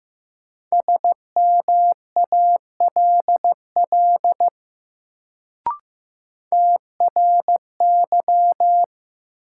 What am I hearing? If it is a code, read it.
Morse code, heard as a keyed tone, about 15 wpm, 700 Hz. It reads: SMALL TRY
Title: Sentences from Top 200 Words - New Word Drill - 15wpm